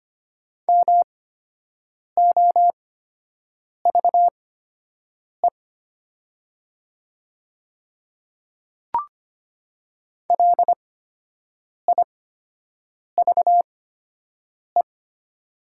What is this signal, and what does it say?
Morse code, heard as a keyed tone, about 25 wpm, spaced as for 7 wpm, 700 Hz. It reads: MOVE LIVE
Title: 8x Character Spacing - Top 300 Words - 25wpm